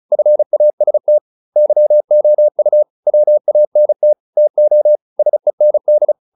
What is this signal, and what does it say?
Morse code, heard as a keyed tone, about 35 wpm, 600 Hz. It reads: FAST YOU WANT TO SEND